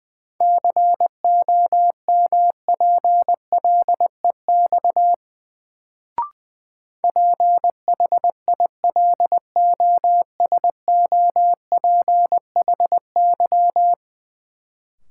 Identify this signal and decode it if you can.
Morse code, heard as a keyed tone, about 20 wpm, 700 Hz. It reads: COMPLEX PHILOSOPHY
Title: Sentences from Top 900 Words - New Word Drill - 20wpm